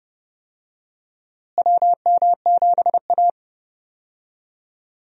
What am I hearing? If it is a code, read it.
Morse code, heard as a keyed tone, about 29 wpm, 700 Hz. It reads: WM7A
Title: USA Call Signs - 4 Characters - Repeated 3x - 30wpm